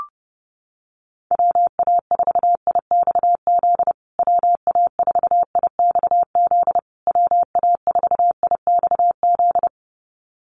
Morse code, heard as a keyed tone, about 30 wpm, 700 Hz. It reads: WA4IXZ WA4IXZ WA4IXZ
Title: USA Call Signs Repeated 3x 30wpm